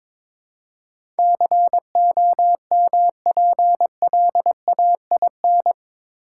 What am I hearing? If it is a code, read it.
Morse code, heard as a keyed tone, about 22 wpm, 700 Hz. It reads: COMPLAIN
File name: practice / 22wpm / Eight Letter Words - 22wpm